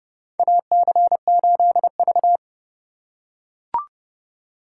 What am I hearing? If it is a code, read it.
Morse code, heard as a keyed tone, about 32 wpm, 700 Hz. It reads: AC8V
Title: USA Call Signs - 4 Characters - 4x Character Spacing - 30wpm